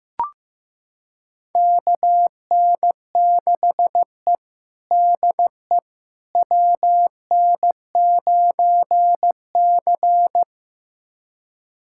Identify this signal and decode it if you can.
Morse code, heard as a keyed tone, about 15 wpm, 700 Hz. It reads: KN6E DE WN9C
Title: USA Callsign DE Callsign - 4 Characters - 15wpm